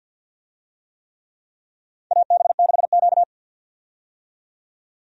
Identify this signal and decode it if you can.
Morse code, heard as a keyed tone, about 52 wpm, 700 Hz. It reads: AB6Q
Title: USA Call Signs - 4 Characters - 50wpm